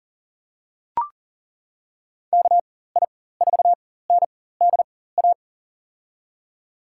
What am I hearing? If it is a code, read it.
Morse code, heard as a keyed tone, about 40 wpm, spaced as for 19 wpm, 700 Hz. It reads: KI4NDA
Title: USA Call Signs - 6 Characters - 4x Character Spacing - 40wpm